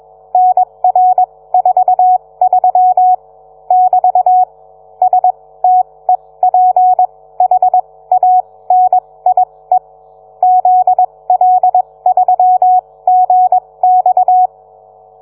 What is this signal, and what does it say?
Morse code, heard as a keyed tone, about 21 wpm, spaced as for 17 wpm, 740 Hz. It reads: NR43 <BT> STEPHANIE ZL3GX